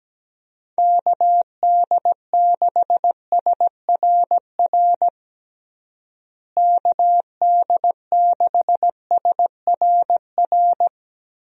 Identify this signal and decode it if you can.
Morse code, heard as a keyed tone, about 17 wpm, 700 Hz. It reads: KD6SRR KD6SRR
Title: USA Call Signs - 6 Characters - Repeated 3x - 17wpm